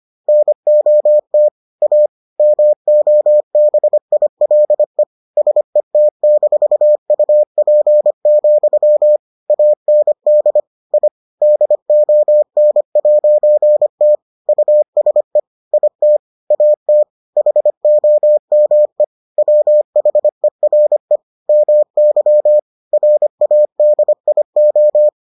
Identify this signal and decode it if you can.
Morse code, heard as a keyed tone, about 25 wpm, 600 Hz. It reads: NOT A MOBILE SET-UP, AND I DON'T USE IT AT HOME WHERE MY RADIO